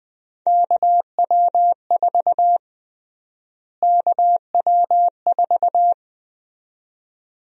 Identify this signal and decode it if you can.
Morse code, heard as a keyed tone, about 21 wpm, 700 Hz. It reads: KW4 KW4
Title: USA Call Sign Prefixes - Repeated 2x - 20wpm